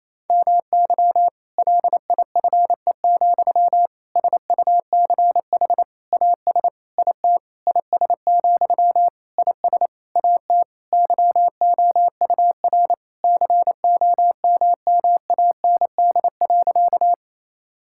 Morse code, heard as a keyed tone, about 27 wpm, 700 Hz. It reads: MY LIFE, SUCH AS IT IS, IS AT YOUR COMMAND.
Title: Sentences from Top 1600 Words 28wpm